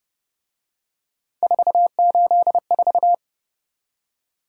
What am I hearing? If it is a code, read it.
Morse code, heard as a keyed tone, about 29 wpm, 700 Hz. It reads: 484
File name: Speed-Racing - Three Letter-Number Combinations - 25wpm